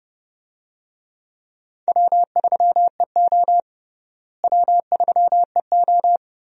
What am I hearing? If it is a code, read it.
Morse code, heard as a keyed tone, about 31 wpm, 700 Hz. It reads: W3EO W3EO
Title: USA Call Signs - 4 Characters - Repeated 3x - 30wpm